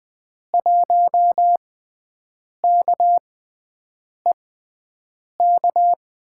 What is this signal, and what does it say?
Morse code, heard as a keyed tone, about 20 wpm, spaced as for 7 wpm, 700 Hz. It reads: 1KEK
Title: USA Call Signs - 6x Character Spacing - 20wpm